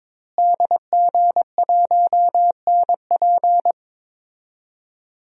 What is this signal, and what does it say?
Morse code, heard as a keyed tone, about 22 wpm, 700 Hz. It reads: DG1NP
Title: DXCC-100 Call Signs - 5 Characters - Repeated 2x - 22wpm